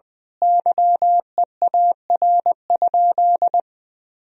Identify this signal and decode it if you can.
Morse code, heard as a keyed tone, about 20 wpm, 700 Hz. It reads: YEAR?
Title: Sentences from Top 1000 Words - Review Part 4 - 20wpm